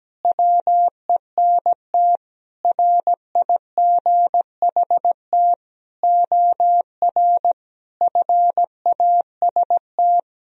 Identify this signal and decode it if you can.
Morse code, heard as a keyed tone, about 17 wpm, 700 Hz. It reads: WENT RIGHT OR FAST
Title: Sentences from Top 1000 Words - Review Part 2 - 17wpm